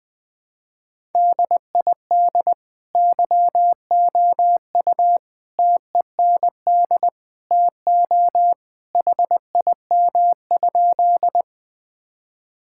Morse code, heard as a keyed tone, about 20 wpm, 700 Hz. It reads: DID YOU TEND TO HIM?
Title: Sentences from Top 1000 Words - Review Part 5 - 20wpm